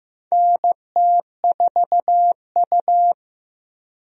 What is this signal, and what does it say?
Morse code, heard as a keyed tone, about 15 wpm, 700 Hz. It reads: NT4U
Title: USA Call Signs - 4 Characters - 15wpm